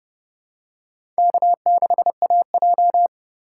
Morse code, heard as a keyed tone, about 30 wpm, 700 Hz. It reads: K6AJ
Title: USA Call Signs - 4 Characters - 4x Character Spacing - 30wpm